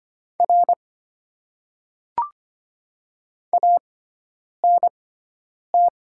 Morse code, heard as a keyed tone, about 25 wpm, spaced as for 9 wpm, 700 Hz. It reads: R ANT
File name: Top 90 QSO Elements - 6x Character Spacing - 25wpm